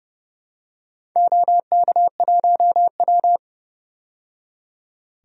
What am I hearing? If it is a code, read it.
Morse code, heard as a keyed tone, about 30 wpm, 700 Hz. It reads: OK1W
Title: DXCC-100 Call Signs - 4 Characters - 30wpm